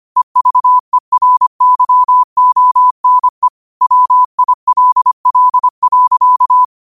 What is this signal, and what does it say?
Morse code, heard as a keyed tone, about 25 wpm, 1000 Hz. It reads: EVERYONE WILL.